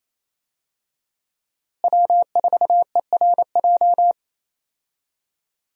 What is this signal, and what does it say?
Morse code, heard as a keyed tone, about 28 wpm, 700 Hz. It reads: W4ERJ
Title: USA Call Signs - 5 Characters - Repeated 3x - 28wpm